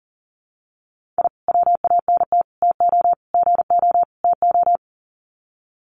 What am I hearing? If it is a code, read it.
Morse code, heard as a keyed tone, about 40 wpm, 700 Hz. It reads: I WANT TO GO TO
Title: Sets of 5 Words Encore 40wpm